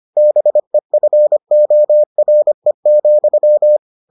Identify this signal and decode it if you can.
Morse code, heard as a keyed tone, about 25 wpm, 600 Hz. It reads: BEFORE,